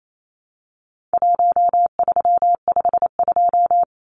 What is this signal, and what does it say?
Morse code, heard as a keyed tone, about 28 wpm, 700 Hz. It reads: 1352
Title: Numbers - Contest Style - 4-Digit 28wpm